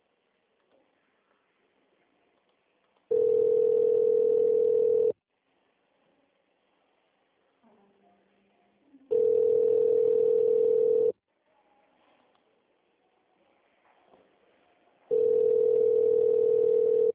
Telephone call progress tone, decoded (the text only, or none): ringback tone